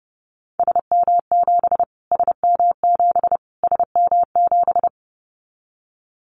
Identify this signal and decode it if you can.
Morse code, heard as a keyed tone, about 30 wpm, 690 Hz. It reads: SM7 SM7 SM7
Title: Callsign Prefixes for Top 25 DXCC - Repeated 3x - 30wpm